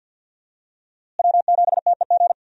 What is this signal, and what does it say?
Morse code, heard as a keyed tone, about 47 wpm, 700 Hz. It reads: W7TEG
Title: USA Call Signs - 5 Characters - Repeated 3x - 50wpm